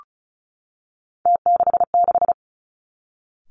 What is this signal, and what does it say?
Morse code, heard as a keyed tone, about 36 wpm, 700 Hz. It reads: T66
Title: Three Challenging Letter-Number Combinations 35wpm